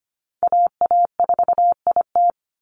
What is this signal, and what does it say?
Morse code, heard as a keyed tone, about 25 wpm, 700 Hz. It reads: AA4IT
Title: NA SOTA Chaser Callsigns 25wpm